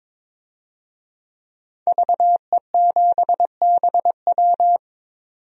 Morse code, heard as a keyed tone, about 22 wpm, 700 Hz. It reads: VE7BW
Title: DXCC-100 Call Signs - 5 Characters - 22wpm